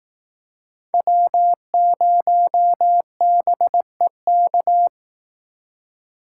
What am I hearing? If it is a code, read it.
Morse code, heard as a keyed tone, about 18 wpm, 700 Hz. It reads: W0BEK
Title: Speed Racing - USA 1x3 Call Signs 15wpm